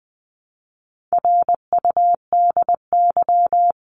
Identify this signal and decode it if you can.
Morse code, heard as a keyed tone, about 20 wpm, 700 Hz. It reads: RUDY
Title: CWT Exchange 20wpm